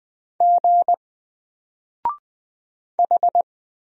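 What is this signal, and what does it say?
Morse code, heard as a keyed tone, about 20 wpm, 700 Hz. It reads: G H
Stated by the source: Single letter and number Morse drill at 20 wpm, from N4LCD.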